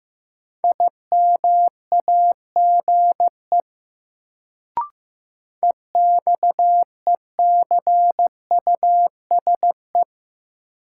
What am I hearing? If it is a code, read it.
Morse code, heard as a keyed tone, about 15 wpm, 700 Hz. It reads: IMAGE EXECUSE
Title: Sentences from Top 700 Words - New Word Drill - 15wpm